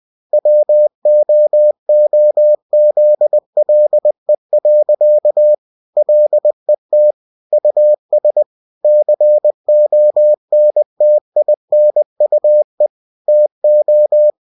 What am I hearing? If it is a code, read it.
Morse code, heard as a keyed tone, about 20 wpm, 600 Hz. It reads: WOOZLE. LET US CONTINUE TO
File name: cw-books / winnie-the-pooh / winnie-20wpm-page-0023